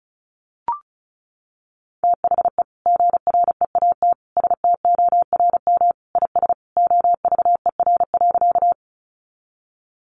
Morse code, heard as a keyed tone, about 34 wpm, 700 Hz. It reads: THE GREAT STORM IS OVER.
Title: Sentences from Top 1500 Words 35wpm